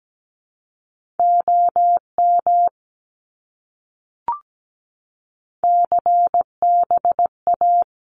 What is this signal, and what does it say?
Morse code, heard as a keyed tone, about 17 wpm, 700 Hz. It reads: OM CBA
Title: Abbreviations Repeated 3x - 17wpm